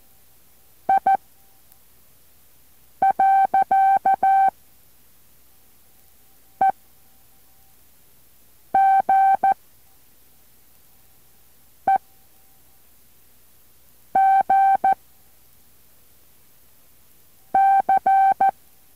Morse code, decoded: I . E G E G C